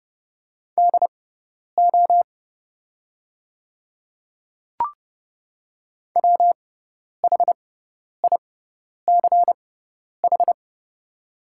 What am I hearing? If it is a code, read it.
Morse code, heard as a keyed tone, about 30 wpm, spaced as for 10 wpm, 700 Hz. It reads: DO WHICH
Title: 6x Character Spacing - Top 100 Words - 30wpm